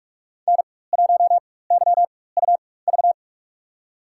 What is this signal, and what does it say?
Morse code, heard as a keyed tone, about 43 wpm, spaced as for 21 wpm, 700 Hz. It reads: N1YUV
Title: USA Call Signs - 4x Character Spacing - 45wpm